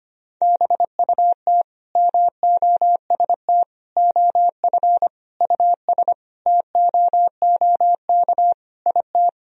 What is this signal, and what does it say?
Morse code, heard as a keyed tone, about 25 wpm, 700 Hz. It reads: BUT MOST OF US TOOK IT
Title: Sentences from Top 1000 Words - Review Part 1 - 25wpm